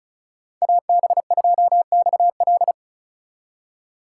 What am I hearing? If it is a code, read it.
Morse code, heard as a keyed tone, about 33 wpm, 700 Hz. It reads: AB2XL
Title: Speed Racing - USA 2x2 Call Signs 35wpm